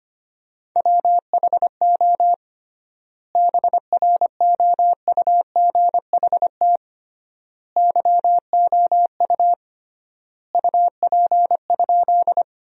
Morse code, heard as a keyed tone, about 25 wpm, 700 Hz. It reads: WHO BROUGHT YOU UP?
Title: 3x Word-Spacing - Sentences from Top 500 Words - 25wpm